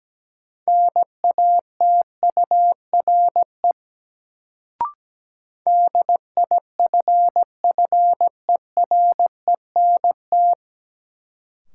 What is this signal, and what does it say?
Morse code, heard as a keyed tone, about 17 wpm, 700 Hz. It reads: NATURE DIFFERENT
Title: Sentences from Top 600 Words - New Word Drill - 17wpm